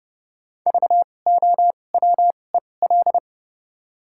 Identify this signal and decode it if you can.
Morse code, heard as a keyed tone, about 30 wpm, 700 Hz. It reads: V O W E L
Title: 2x Character Spacing - Top 500 Words - 30wpm